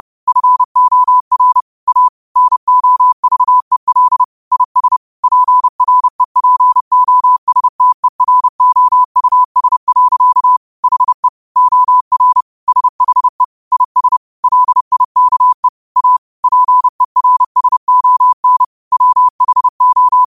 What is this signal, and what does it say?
Morse code, heard as a keyed tone, about 30 wpm, 1000 Hz. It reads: FOR A NOVEL IS PREPOSTEROUS. HE OR SHE IS LIKE A PERSON WHO
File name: qotdfiles / 30wpm